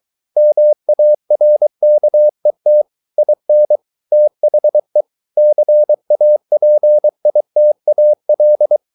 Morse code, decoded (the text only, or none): MARKET IN THE CAPITAL